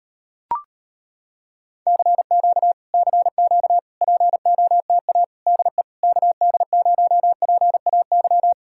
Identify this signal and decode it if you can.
Morse code, heard as a keyed tone, about 35 wpm, 700 Hz. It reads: CQ CQ POTA DE KD0PAY